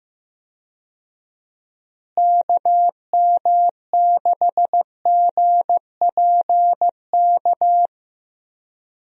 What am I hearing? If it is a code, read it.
Morse code, heard as a keyed tone, about 15 wpm, 700 Hz. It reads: KM6GPK
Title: USA Call Signs - 6 Characters - Repeated 2x - 15wpm